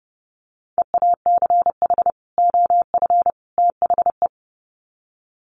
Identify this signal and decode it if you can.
Morse code, heard as a keyed tone, about 30 wpm, 700 Hz. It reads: EACH OF THE
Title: Sets of 3 Words 30wpm